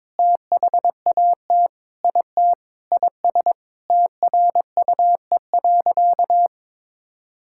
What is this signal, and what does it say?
Morse code, heard as a keyed tone, about 22 wpm, 700 Hz. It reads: THAT IT IS TRUE.
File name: practice / 22wpm / Sentences with 3-Word Phrases - 22wpm